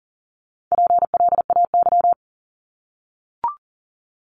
Morse code, heard as a keyed tone, about 42 wpm, 700 Hz. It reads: PLAY
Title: Multi-letter Phonemes 40wpm